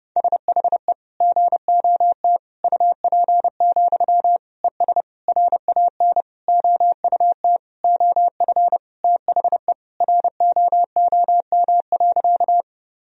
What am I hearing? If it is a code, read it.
Morse code, heard as a keyed tone, about 30 wpm, 700 Hz. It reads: SHE GOT UP, ES RAN OUT OF THE ROOM.
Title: Sentences from Top 500 Words and Abbreviations 30wpm